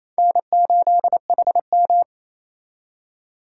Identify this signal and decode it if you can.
Morse code, heard as a keyed tone, about 36 wpm, 700 Hz. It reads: N8HM